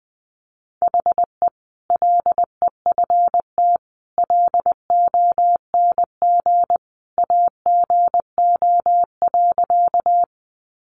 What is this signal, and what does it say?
Morse code, heard as a keyed tone, about 20 wpm, 700 Hz. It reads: HE LEFT LONG AGO.
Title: Sentences from Top 500 Words 20wpm